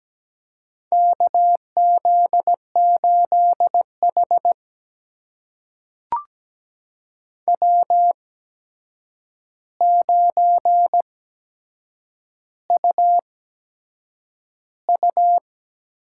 Morse code, heard as a keyed tone, about 17 wpm, spaced as for 7 wpm, 700 Hz. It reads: KZ8H W9UU
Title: USA Call Signs - 4 Characters - 8x Character Spacing - 17wpm